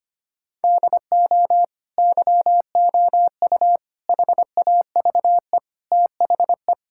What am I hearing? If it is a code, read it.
Morse code, heard as a keyed tone, about 25 wpm, 700 Hz. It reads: DO YOU HAVE THE
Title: Sentences from Top 1000 Words - Review Part 6 - 25wpm